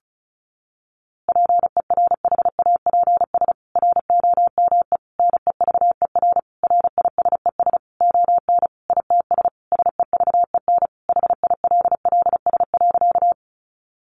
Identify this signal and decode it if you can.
Morse code, heard as a keyed tone, about 34 wpm, 700 Hz. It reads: PERHAPS ROME NEVER RISES ON ITS SEVEN HILLS.
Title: A Sound Of Thunder 35wpm